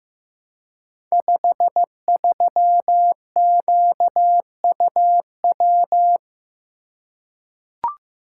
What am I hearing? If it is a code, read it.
Morse code, heard as a keyed tone, about 15 wpm, 700 Hz. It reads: 53QUW
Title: Code Groups - 5 Letter:Numbers - 15wpm